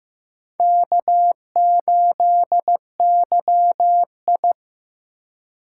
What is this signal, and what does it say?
Morse code, heard as a keyed tone, about 15 wpm, 700 Hz. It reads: K8YI
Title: USA Call Signs - 4 Characters - 2x Character Spacing - 15wpm